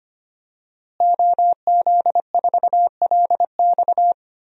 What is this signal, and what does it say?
Morse code, heard as a keyed tone, about 25 wpm, 700 Hz. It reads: OZ4LX
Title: DXCC-100 Call Signs - 5 Characters - Repeated 2x - 25wpm